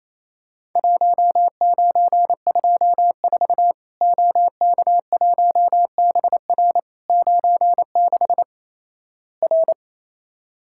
Morse code, heard as a keyed tone, about 28 wpm, 700 Hz. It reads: 1924 OK1BR 96 R
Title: Worked All Europe DX Contest - QTC Exchange - 28wpm